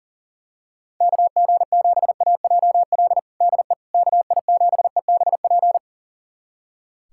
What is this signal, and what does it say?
Morse code, heard as a keyed tone, about 40 wpm, 690 Hz. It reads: KG7AJL DE KI7EBP
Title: USA Callsign DE Callsign - 6 Characters - 40wpm